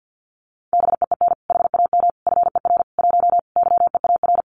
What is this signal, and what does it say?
Morse code, heard as a keyed tone, about 50 wpm, 700 Hz. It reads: BEEN HAM FER 1 YEAR
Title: Common QSO Phrases Encore 50wpm